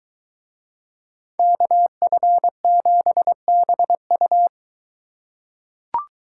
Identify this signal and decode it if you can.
Morse code, heard as a keyed tone, about 23 wpm, 700 Hz. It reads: KF7BU